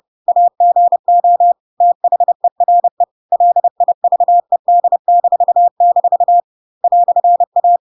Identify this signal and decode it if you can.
Morse code, heard as a keyed tone, about 30 wpm, 700 Hz. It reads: AGO THERE LIVED-- "A
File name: Pinocchio / Pinocchio ch1 30 30 ik1hge